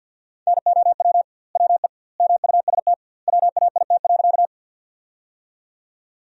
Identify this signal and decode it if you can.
Morse code, heard as a keyed tone, about 47 wpm, 700 Hz. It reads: NOW WE MUST WAIT.
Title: 2x Word-Spacing - Sentences from Top 500 Words - 50wpm